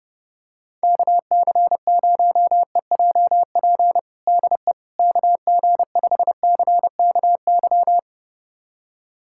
Morse code, heard as a keyed tone, about 31 wpm, 700 Hz. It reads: KC0EJP DE KG5CKY
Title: USA Callsign DE Callsign - 6 Characters - 30wpm